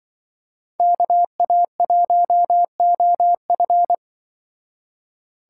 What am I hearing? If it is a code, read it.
Morse code, heard as a keyed tone, about 25 wpm, 700 Hz. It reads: KA1OF
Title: Speed Racing - USA 2x2 Call Signs 20wpm